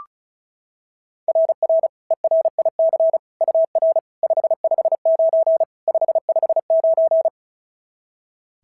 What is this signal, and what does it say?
Morse code, heard as a keyed tone, about 36 wpm, 690 Hz. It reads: RR ERIC UR 559 559
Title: SOTA QSO Practice 35wpm